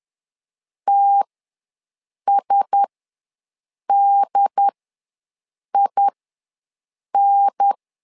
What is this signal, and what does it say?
Morse code, heard as a keyed tone, about 10 wpm, spaced as for 6 wpm, 800 Hz. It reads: TSDIN
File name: LEZIONI / 4 Q1S ws 10wpm 2208